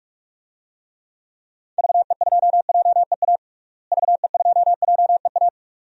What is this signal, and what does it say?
Morse code, heard as a keyed tone, about 42 wpm, 700 Hz. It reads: VE2JEA VE2JEA
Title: DXCC-100 Call Signs - 6 Characters - Repeated 2x - 45wpm